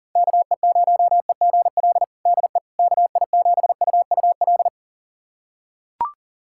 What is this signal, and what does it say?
Morse code, heard as a keyed tone, about 37 wpm, 700 Hz. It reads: KE0EGL DE KI7UUL
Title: USA Callsign DE Callsign 40wpm